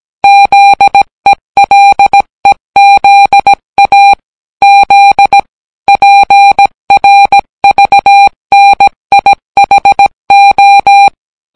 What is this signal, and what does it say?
Morse code, heard as a keyed tone, about 17 wpm, 800 Hz. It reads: ZELEZA Z PRVNIHO